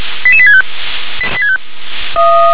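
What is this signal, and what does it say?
Five-tone selective call (ZVEI), heard as quick runs of five tones, tones of about 70 ms. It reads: 88865
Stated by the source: ZVEI five-tone selective call received by SDR.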